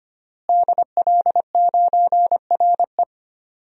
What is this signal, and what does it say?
Morse code, heard as a keyed tone, about 25 wpm, 700 Hz. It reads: DL9RE
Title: DXCC-100 Call Signs - 5 Characters - Repeated 2x - 25wpm